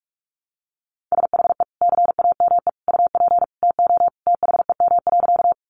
Morse code, heard as a keyed tone, about 44 wpm, 700 Hz. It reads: SHE CAME UP TO THEM.
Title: Sentences from Top 200 Words 45wpm